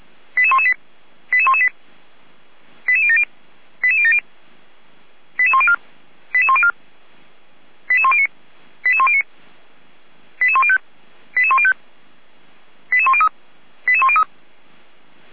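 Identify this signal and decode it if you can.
Five-tone selective call (ZVEI), heard as quick runs of five tones, tones of about 70 ms. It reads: 88118 88118 88070 88070 88104 88104 88109 88109 88105 88105 88103 88103
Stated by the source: ZVEI five-tone selective call received with an SDR.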